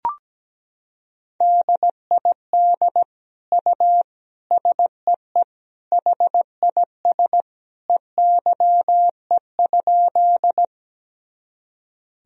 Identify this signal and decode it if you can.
Morse code, heard as a keyed tone, about 17 wpm, 700 Hz. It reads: DID U SEE HIS EYE?
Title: Sentences from Top 300 Words and Abbreviations 17wpm